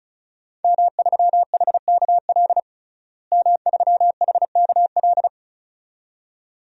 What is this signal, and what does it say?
Morse code, heard as a keyed tone, about 35 wpm, 700 Hz. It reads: M3HKL M3HKL
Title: DXCC-100 Call Signs - 5 Characters - Repeated 2x - 35wpm